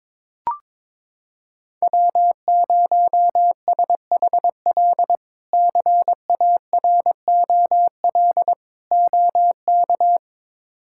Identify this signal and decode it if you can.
Morse code, heard as a keyed tone, about 22 wpm, 700 Hz. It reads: W0SHL CAROL OK
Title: K1USN SST QSO Practice - 22wpm